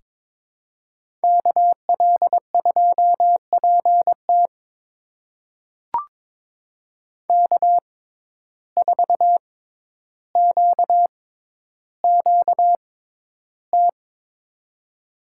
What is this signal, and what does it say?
Morse code, heard as a keyed tone, about 23 wpm, spaced as for 11 wpm, 700 Hz. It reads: KL2PT K4QQT
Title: USA Call Signs - 5 Characters - 6x Character Spacing - 22wpm